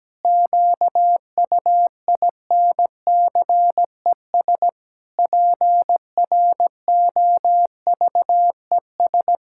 Morse code, heard as a keyed tone, about 17 wpm, 700 Hz. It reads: QUINCES PROVES
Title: Pangram Challenge v2 - 17wpm